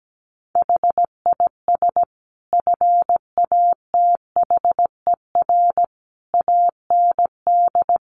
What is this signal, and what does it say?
Morse code, heard as a keyed tone, about 17 wpm, 700 Hz. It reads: HIS FATHER AND
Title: Sentences from Top 1500 Words 17wpm